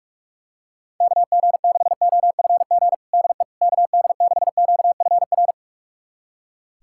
Morse code, heard as a keyed tone, about 42 wpm, 700 Hz. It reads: KG6OFG DE KD6QFR
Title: USA Callsign DE Callsign 45wpm